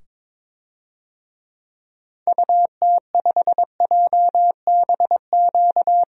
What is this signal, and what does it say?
Morse code, heard as a keyed tone, about 22 wpm, 700 Hz. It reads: UT5JBQ
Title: DXCC-100 Call Signs - 6 Characters - 22wpm